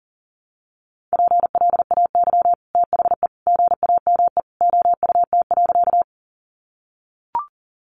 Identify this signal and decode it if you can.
Morse code, heard as a keyed tone, about 40 wpm, 700 Hz. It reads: PLAY THE GAME OUT.
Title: Sentences from Top 500 Words 40wpm